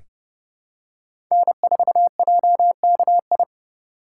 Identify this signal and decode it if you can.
Morse code, heard as a keyed tone, about 30 wpm, 700 Hz. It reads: N4JKI
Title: USA Call Signs - 5 Characters - 3x Character Spacing - 30wpm